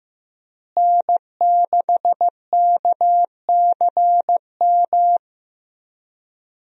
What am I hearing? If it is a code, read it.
Morse code, heard as a keyed tone, about 15 wpm, 700 Hz. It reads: N6KCM
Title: USA Call Signs - 5 Characters - Repeated 3x - 15wpm